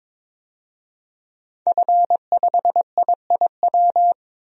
Morse code, heard as a keyed tone, about 23 wpm, 700 Hz. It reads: F5IIW
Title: DXCC-100 Call Signs - 5 Characters - 22wpm